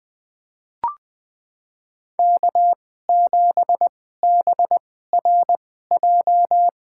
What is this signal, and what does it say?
Morse code, heard as a keyed tone, about 20 wpm, spaced as for 15 wpm, 700 Hz. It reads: K7BRJ
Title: USA Call Signs - 2x Character Spacing - 20wpm